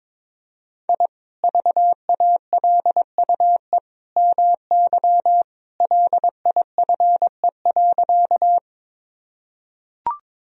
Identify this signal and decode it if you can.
Morse code, heard as a keyed tone, about 22 wpm, 700 Hz. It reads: I VALUE MY LIFE.
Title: Sentences from Top 1000 Words - Review Part 3 - 22wpm